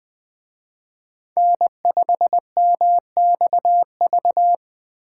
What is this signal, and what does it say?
Morse code, heard as a keyed tone, about 20 wpm, 700 Hz. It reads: N5MXV